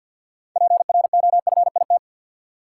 Morse code, heard as a keyed tone, about 47 wpm, 700 Hz. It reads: PROFIT
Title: Sentences from Top 800 Words - New Word Drill - 50wpm